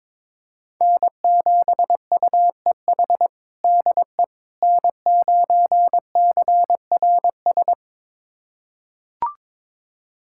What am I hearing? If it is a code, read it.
Morse code, heard as a keyed tone, about 21 wpm, 700 Hz. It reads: N7UEH DE N9CRS
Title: USA Callsign DE Callsign - 5 Characters - 22wpm